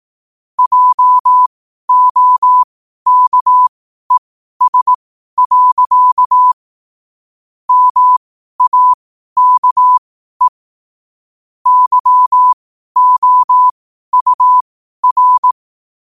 Morse code, heard as a keyed tone, about 18 wpm, spaced as for 12 wpm, 1000 Hz. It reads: JOKES. MAKE YOUR
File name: qotdfiles / 12wpm